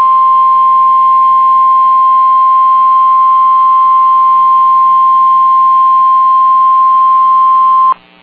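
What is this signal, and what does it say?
Emergency Alert System (SAME) tones (the 1050 Hz attention tone). It is heard from 1.1 to 7.1 s.
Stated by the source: SAME tones from Weather Radio KGG95 (Winona, MN), National Weather Service office in La Crosse.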